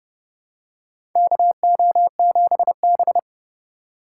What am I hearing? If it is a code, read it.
Morse code, heard as a keyed tone, about 30 wpm, 700 Hz. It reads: KO7B
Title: USA Call Signs - 2x Character Spacing - 30wpm